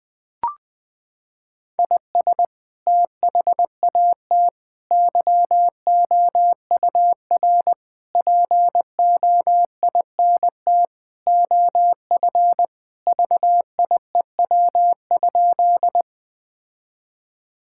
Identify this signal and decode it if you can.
Morse code, heard as a keyed tone, about 20 wpm, 700 Hz. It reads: IS THAT YOUR POINT OF VIEW?
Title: Sentences from Top 1000 Words - Review Part 6 - 20wpm